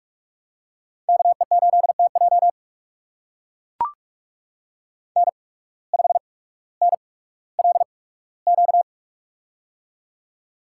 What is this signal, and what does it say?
Morse code, heard as a keyed tone, about 44 wpm, spaced as for 19 wpm, 690 Hz. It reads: KE8TJ N5NLQ
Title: USA Call Signs - 5 Characters - 8x Character Spacing - 45wpm